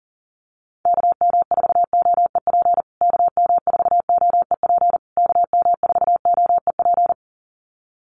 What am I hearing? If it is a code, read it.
Morse code, heard as a keyed tone, about 40 wpm, 700 Hz. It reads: KM4OEP KM4OEP KM4OEP
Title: USA Call Signs Repeated 3x 40wpm